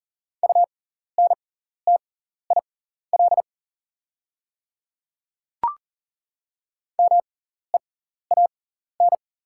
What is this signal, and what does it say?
Morse code, heard as a keyed tone, about 40 wpm, spaced as for 14 wpm, 690 Hz. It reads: UNTIL MEAN
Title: 6x Character Spacing - Top 300 Words - 40wpm